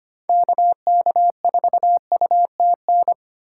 Morse code, heard as a keyed tone, about 25 wpm, 700 Hz. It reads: KK4UTN
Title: USA Call Signs - 4x Character Spacing - 25wpm